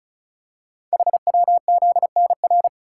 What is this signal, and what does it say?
Morse code, heard as a keyed tone, about 36 wpm, 700 Hz. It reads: HWZNR
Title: Code Groups - 5 Letter:Numbers - 35wpm